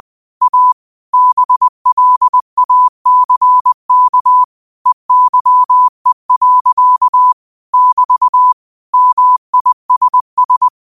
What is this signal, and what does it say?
Morse code, heard as a keyed tone, about 20 wpm, 1000 Hz. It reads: A BLACK EYE. <BT> MISS
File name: qotdfiles / 20wpm